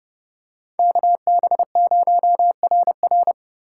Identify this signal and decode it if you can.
Morse code, heard as a keyed tone, about 30 wpm, 700 Hz. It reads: KB0RR
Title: USA Call Signs - 5 Characters - 2x Character Spacing - 30wpm